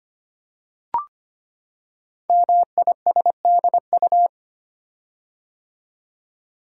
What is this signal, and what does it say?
Morse code, heard as a keyed tone, about 25 wpm, 700 Hz. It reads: MISDU
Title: Code Groups - 5 Letters - 25wpm